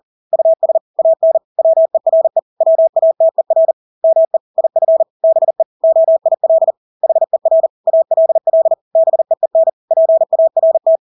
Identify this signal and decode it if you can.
Morse code, heard as a keyed tone, about 40 wpm, 650 Hz. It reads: US AN WERE WATER ME IF BE OIL HER ALL BEEN PART
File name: Engelske ord 40-40 wpm